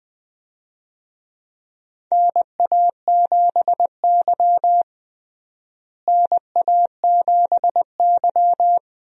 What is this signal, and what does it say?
Morse code, heard as a keyed tone, about 20 wpm, 700 Hz. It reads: NA7Y NA7Y
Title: USA Call Signs - 4 Characters - Repeated 3x - 20wpm